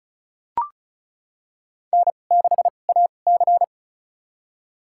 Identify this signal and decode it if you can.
Morse code, heard as a keyed tone, about 36 wpm, spaced as for 25 wpm, 700 Hz. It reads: N6AC
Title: USA Call Signs - 2x Character Spacing - 35wpm